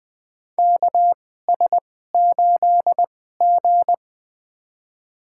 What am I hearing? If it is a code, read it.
Morse code, heard as a keyed tone, about 20 wpm, spaced as for 14 wpm, 700 Hz. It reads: KS8G
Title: USA Call Signs - 4 Characters - 2x Character Spacing - 20wpm